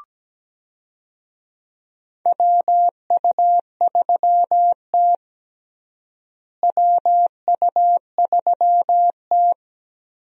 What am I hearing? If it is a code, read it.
Morse code, heard as a keyed tone, about 17 wpm, 700 Hz. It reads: WU3T WU3T
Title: USA Call Signs - 4 Characters - Repeated 2x - 17wpm